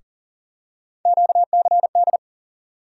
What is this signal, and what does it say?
Morse code, heard as a keyed tone, about 40 wpm, 700 Hz. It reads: QCD
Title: USA Call Sign Postfixes - 3 Letters 40wpm